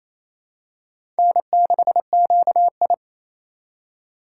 Morse code, heard as a keyed tone, about 28 wpm, 700 Hz. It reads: N6QI
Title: USA Call Signs - 4 Characters - 3x Character Spacing - 28wpm